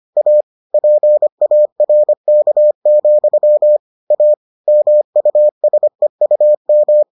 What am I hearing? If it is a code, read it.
Morse code, heard as a keyed tone, about 25 wpm, 600 Hz. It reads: A PARK, A MUSEUM